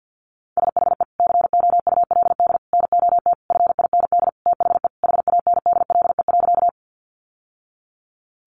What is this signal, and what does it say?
Morse code, heard as a keyed tone, about 50 wpm, 700 Hz. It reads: SHE COULD NOT FIND THE HANDLE.
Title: Sentences from Top 1100 Words 50wpm